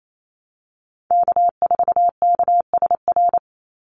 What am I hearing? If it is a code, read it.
Morse code, heard as a keyed tone, about 28 wpm, 700 Hz. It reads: K4KSR
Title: Call Signs Repeated 2x - 28wpm